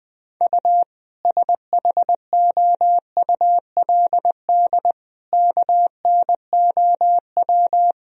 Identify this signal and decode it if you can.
Morse code, heard as a keyed tone, about 20 wpm, 700 Hz. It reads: U SHOULD KNOW
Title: Sentences from Top 400 Words and Abbreviations 20wpm